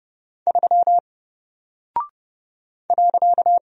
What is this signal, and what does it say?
Morse code, heard as a keyed tone, about 30 wpm, 700 Hz. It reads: 3 .
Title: Single Character (Letter:Number:Punctuation:Prosign) 30wpm